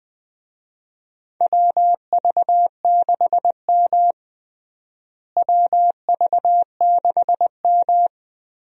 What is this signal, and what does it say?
Morse code, heard as a keyed tone, about 21 wpm, 700 Hz. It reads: WV6M WV6M
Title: USA Call Signs - 4 Characters - Repeated 3x - 20wpm